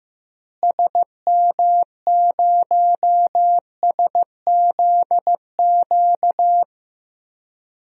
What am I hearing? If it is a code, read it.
Morse code, heard as a keyed tone, about 15 wpm, 700 Hz. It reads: SM0SZQ
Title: DXCC-100 Call Signs - 6 Characters - Repeated 3x - 15wpm